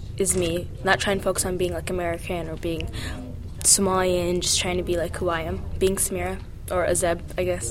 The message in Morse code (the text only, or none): none